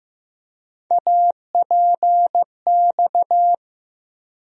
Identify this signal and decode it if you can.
Morse code, heard as a keyed tone, about 15 wpm, 700 Hz. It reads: APX
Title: USA Call Sign Postfixes - 3 Letters 15wpm